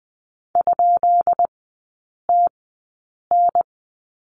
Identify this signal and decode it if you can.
Morse code, heard as a keyed tone, about 20 wpm, spaced as for 8 wpm, 700 Hz. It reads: ?TN
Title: 138 Lesson 23e - Letter f - Three Characters 20wpm